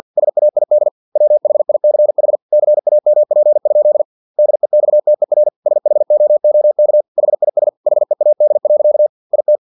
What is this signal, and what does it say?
Morse code, heard as a keyed tone, about 50 wpm, 600 Hz. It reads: SAID WHICH CAMP? DEXTER SHOOK HIS HEAD. IT
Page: Never 49wpm